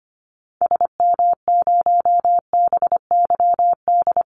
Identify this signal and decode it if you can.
Morse code, heard as a keyed tone, about 25 wpm, 700 Hz. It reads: SM0BYD
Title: Call Signs - 25wpm